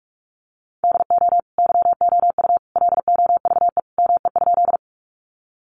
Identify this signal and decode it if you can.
Morse code, heard as a keyed tone, about 44 wpm, 700 Hz. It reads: DO YOU LOVE ME?
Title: Sentences from Top 400 Words 45wpm